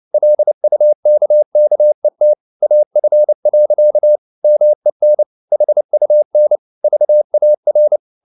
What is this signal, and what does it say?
Morse code, heard as a keyed tone, about 29 wpm, 600 Hz. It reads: LUKKET AF. MEN HUN VAR
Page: CW med Gnister